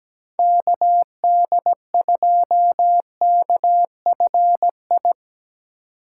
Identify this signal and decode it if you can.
Morse code, heard as a keyed tone, about 20 wpm, 700 Hz. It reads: KD2KFI
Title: Speed Racing - USA Call Signs 17wpm